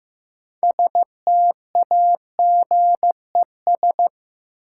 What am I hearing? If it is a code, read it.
Morse code, heard as a keyed tone, about 15 wpm, 700 Hz. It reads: STAGES
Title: Sets of 5 Words Encore - New Word Drill - 15wpm